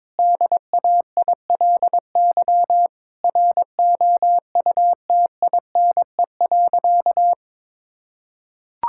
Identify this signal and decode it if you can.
Morse code, heard as a keyed tone, about 22 wpm, 700 Hz. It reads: DAILY ROUTINE.
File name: Difficult Sentences with Hints - 22wpm